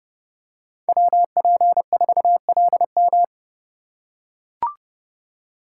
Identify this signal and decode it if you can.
Morse code, heard as a keyed tone, about 30 wpm, 700 Hz. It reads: WP4LM
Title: USA Call Signs - 4x Character Spacing - 30wpm